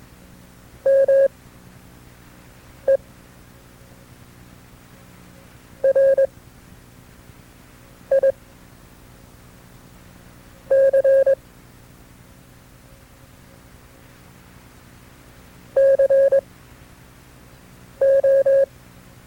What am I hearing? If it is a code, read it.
Morse code, heard as a keyed tone, about 22 wpm, spaced as for 4 wpm, 540 Hz. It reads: MERIC CO
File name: FreeFiles / CW-practice / KB6MT-tape2-side2